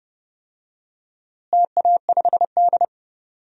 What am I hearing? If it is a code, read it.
Morse code, heard as a keyed tone, about 30 wpm, 700 Hz. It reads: TA5D
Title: DXCC-100 Call Signs - 4 Characters - Repeated 2x - 30wpm